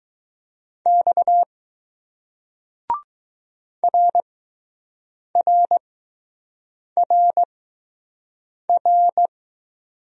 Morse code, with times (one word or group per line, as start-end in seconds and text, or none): 0.9-1.4 X
3.8-4.2 R
5.4-5.8 R
7.0-7.4 R
8.7-9.3 R